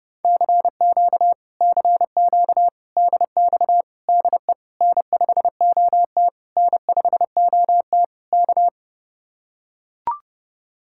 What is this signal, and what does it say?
Morse code, heard as a keyed tone, about 31 wpm, 700 Hz. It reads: CQ CQ DX DE N5OT N5OT K